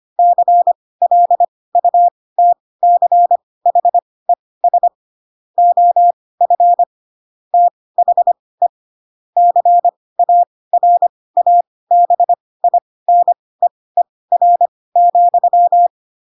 Morse code, heard as a keyed tone, about 25 wpm, spaced as for 18 wpm, 700 Hz. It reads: CLUTCHES OF THE CARABINEER,
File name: Pinocchio / Pinocchio ch4 12 25 ik1hge